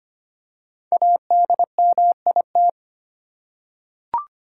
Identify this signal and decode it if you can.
Morse code, heard as a keyed tone, about 25 wpm, 700 Hz. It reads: ADMIT
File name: 5 Letter Words - First 3 Letters Given - 25wpm